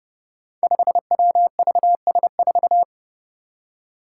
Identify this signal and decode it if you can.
Morse code, heard as a keyed tone, about 29 wpm, 700 Hz. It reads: 5WVS4
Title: Five Challenging Letter-Number Combinations 30wpm